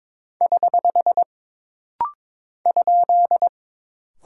Morse code, heard as a keyed tone, about 22 wpm, 700 Hz. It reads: <HH> ?
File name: practice / 22wpm / Single Punctuation-Prosign - Rapid-Fire - 22wpm